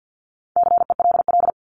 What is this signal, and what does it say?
Morse code, heard as a keyed tone, about 50 wpm, 700 Hz. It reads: CELL
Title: Top 1001-1250 Words 50wpm